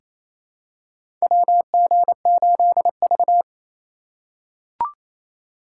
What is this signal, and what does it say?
Morse code, heard as a keyed tone, about 28 wpm, 700 Hz. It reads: WG8V
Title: USA Call Signs - 4x Character Spacing - 28wpm